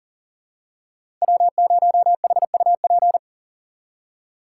Morse code, heard as a keyed tone, about 40 wpm, 700 Hz. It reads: W0HUP
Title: USA Call Signs - 5 Characters - 6x Character Spacing - 40wpm